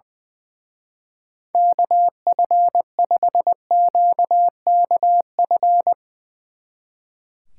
Morse code, heard as a keyed tone, about 21 wpm, 700 Hz. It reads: KF5QKF
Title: Speed Racing - USA 2x3 Call Signs 20wpm